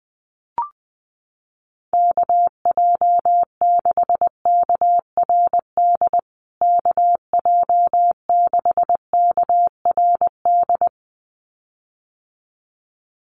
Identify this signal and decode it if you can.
Morse code, heard as a keyed tone, about 20 wpm, 700 Hz. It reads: KJ6KRD KJ6KRD
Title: USA Call Signs Repeated 2x 20wpm